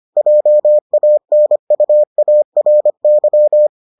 Morse code, heard as a keyed tone, about 25 wpm, 600 Hz. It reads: JANUARY